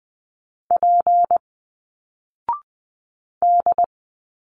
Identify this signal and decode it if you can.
Morse code, heard as a keyed tone, about 20 wpm, 700 Hz. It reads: P D